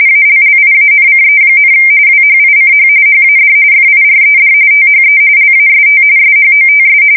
RTTY, about 45.45 baud. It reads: ALLIANCE, WHICH HAD FIRED THE LAST SHOTS
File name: fsk170 / usnavy1